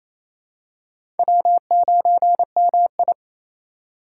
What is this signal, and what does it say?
Morse code, heard as a keyed tone, about 28 wpm, 690 Hz. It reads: W9MI
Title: USA Call Signs - 2x Character Spacing - 28wpm